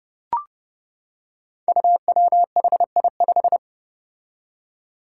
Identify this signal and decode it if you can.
Morse code, heard as a keyed tone, about 29 wpm, 700 Hz. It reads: UWHI5
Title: Five Challenging Letter-Number Combinations 30wpm